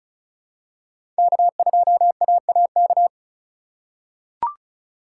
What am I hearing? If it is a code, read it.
Morse code, heard as a keyed tone, about 36 wpm, 700 Hz. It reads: K2AAK
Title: USA Call Signs - 5 Characters - 4x Character Spacing - 35wpm